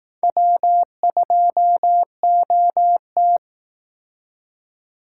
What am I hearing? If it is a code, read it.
Morse code, heard as a keyed tone, about 19 wpm, 700 Hz. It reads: W2OT
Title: Speed Racing - USA 1x2 Call Signs 15wpm